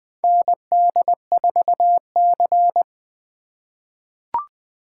Morse code, heard as a keyed tone, about 20 wpm, 700 Hz. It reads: ND4C
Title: USA Call Signs - 4 Characters - 2x Character Spacing - 20wpm